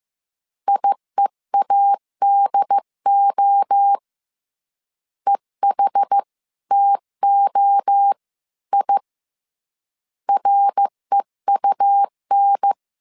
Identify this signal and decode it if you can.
Morse code, heard as a keyed tone, about 15 wpm, 800 Hz. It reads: IEADO EHTO I REUN
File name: CW / DET / 1 3 1500C S C 14wpm 2458